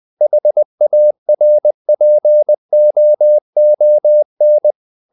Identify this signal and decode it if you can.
Morse code, heard as a keyed tone, about 20 wpm, 600 Hz. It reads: HARPOON
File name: the old man and the sea part 2 20wpm